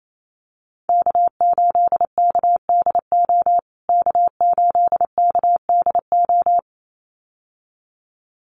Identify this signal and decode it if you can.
Morse code, heard as a keyed tone, about 28 wpm, 700 Hz. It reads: K8KDO K8KDO
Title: USA Call Signs Repeated 2x 28wpm